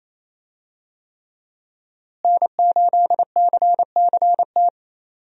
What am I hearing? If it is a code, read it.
Morse code, heard as a keyed tone, about 28 wpm, 700 Hz. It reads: N8CCT
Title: USA Call Signs - 5 Characters - Repeated 3x - 28wpm